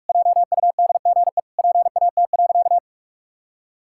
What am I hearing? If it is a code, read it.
Morse code, heard as a keyed tone, about 46 wpm, 700 Hz. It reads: JUDGE PAT.
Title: Pangram Challenge v2 - 45wpm